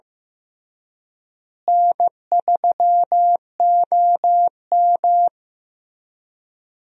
Morse code, heard as a keyed tone, about 15 wpm, 700 Hz. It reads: N3OM
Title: USA Call Signs - 4 Characters - Repeated 3x - 15wpm